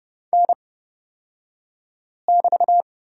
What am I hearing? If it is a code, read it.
Morse code, heard as a keyed tone, about 29 wpm, spaced as for 5 wpm, 700 Hz. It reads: N<BT>
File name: Lesson 36d - BT - Two Characters 30wpm